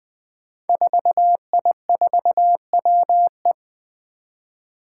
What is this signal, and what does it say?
Morse code, heard as a keyed tone, about 20 wpm, 700 Hz. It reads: 4I4WE
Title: Five Challenging Letter-Number Combinations 20wpm